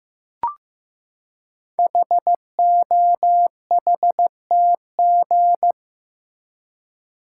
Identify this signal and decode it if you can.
Morse code, heard as a keyed tone, about 15 wpm, 700 Hz. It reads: HOHTG
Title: Code Groups - 5 Letters - 15wpm